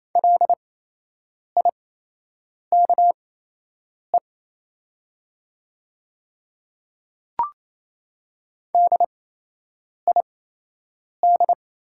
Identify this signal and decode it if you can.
Morse code, heard as a keyed tone, about 28 wpm, spaced as for 8 wpm, 700 Hz. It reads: LIKE DID
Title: 8x Character Spacing - Top 100 Words - 28wpm